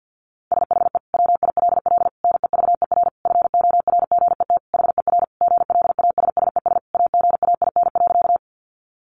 Morse code, heard as a keyed tone, about 50 wpm, 700 Hz. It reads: SHE WILL NEVER FORGET HER GLASSES AGAIN.
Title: Additional Easy Phrases 50wpm